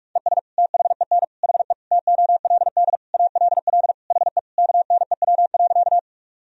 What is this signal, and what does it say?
Morse code, heard as a keyed tone, about 41 wpm, 700 Hz. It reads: ES THEN HE TOLD ALL HE KNEW.
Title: Sentences from Top 400 Words and Abbreviations 45wpm